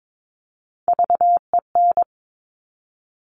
Morse code, heard as a keyed tone, about 22 wpm, 700 Hz. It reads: VEN
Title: Top Three Letter Combinations in Words 22wpm